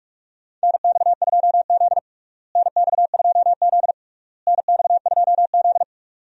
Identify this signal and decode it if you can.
Morse code, heard as a keyed tone, about 46 wpm, 700 Hz. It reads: NX2Z NX2Z NX2Z
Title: USA Call Signs - 4 Characters - Repeated 3x - 45wpm